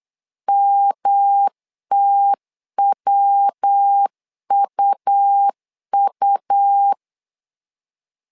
Morse code, heard as a keyed tone, about 8 wpm, 800 Hz. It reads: MTWUU
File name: LEZIONI / 3 MWRO / 3 MW 8wpm 2501